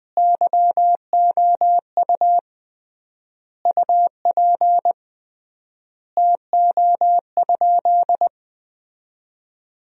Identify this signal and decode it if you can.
Morse code, heard as a keyed tone, about 21 wpm, 700 Hz. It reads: YOU UP TO?
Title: 3x Word-Spacing - Sentences from Top 100 Words - 20wpm